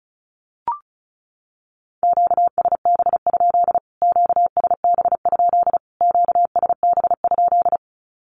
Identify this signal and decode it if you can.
Morse code, heard as a keyed tone, about 34 wpm, 700 Hz. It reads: QSB? QSB? QSB?
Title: Q-Codes Repeated 3x 35wpm